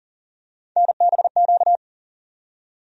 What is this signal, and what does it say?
Morse code, heard as a keyed tone, about 40 wpm, 700 Hz. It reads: NBQ
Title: USA Call Sign Postfixes - 3 Letters 40wpm